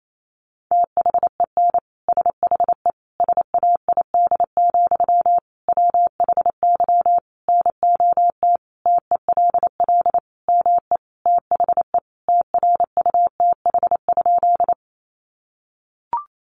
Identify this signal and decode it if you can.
Morse code, heard as a keyed tone, about 28 wpm, 700 Hz. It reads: THEN SHE SAID, WHY NOT TELL ME THE TRUTH?
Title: Sentences from Top 800 Words 28wpm